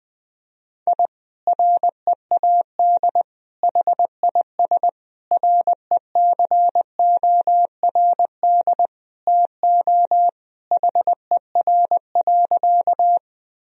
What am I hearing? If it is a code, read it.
Morse code, heard as a keyed tone, about 20 wpm, 700 Hz. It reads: I READ HIS RECORD TO HER.
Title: Sentences from Top 1000 Words - Review Part 3 - 20wpm